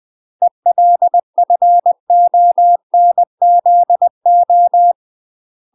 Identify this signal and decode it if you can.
Morse code, heard as a keyed tone, about 20 wpm, 700 Hz. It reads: ELFONZO
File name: UK MASTER QSOs TRAINING / 20WPM / UK MASTER QSOs 0023